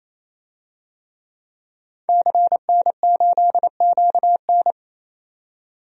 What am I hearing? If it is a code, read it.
Morse code, heard as a keyed tone, about 28 wpm, 700 Hz. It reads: CN8QN
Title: DXCC-100 Call Signs - 5 Characters - 28wpm